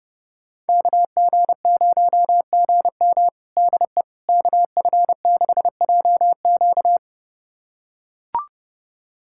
Morse code, heard as a keyed tone, about 30 wpm, 700 Hz. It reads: KG0GM DE KF6JQ
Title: USA Callsign DE Callsign - 5 Characters - 30wpm